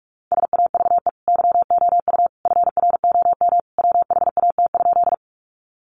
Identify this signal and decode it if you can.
Morse code, heard as a keyed tone, about 44 wpm, 700 Hz. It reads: SAVE YOU FROM WHAT?
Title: Sentences from Top 1100 Words 45wpm